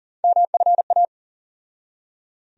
Morse code, heard as a keyed tone, about 40 wpm, 700 Hz. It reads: MFA
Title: USA Call Sign Postfixes - 3 Letters 40wpm